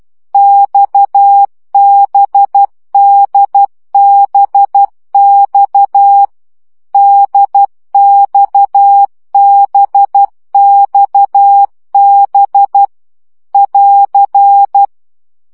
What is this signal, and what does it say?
Morse code, heard as a keyed tone, about 12 wpm, 800 Hz. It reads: XBDBX DXBXB <AR>
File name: CW gyakorlas d b x 10 baud